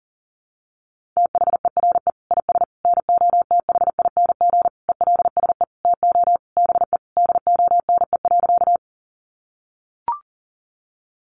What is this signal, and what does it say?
Morse code, heard as a keyed tone, about 40 wpm, 700 Hz. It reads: THERE IS NOTHING ELSE TO BE DONE.
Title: Sentences from Top 800 Words 40wpm